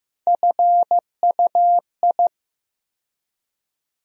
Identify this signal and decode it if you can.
Morse code, heard as a keyed tone, about 15 wpm, 700 Hz. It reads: FUI
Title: USA Call Sign Postfixes - 3 Letters 15wpm